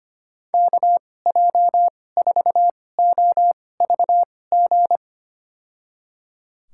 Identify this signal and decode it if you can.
Morse code, heard as a keyed tone, about 25 wpm, spaced as for 18 wpm, 700 Hz. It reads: KJ4OVG
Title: USA Call Signs - 2x Character Spacing - 25wpm